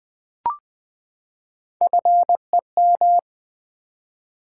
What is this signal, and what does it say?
Morse code, heard as a keyed tone, about 20 wpm, 700 Hz. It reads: FEM
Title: Build A Word - 3 Letters At A Time - 8-Letter Words - 20wpm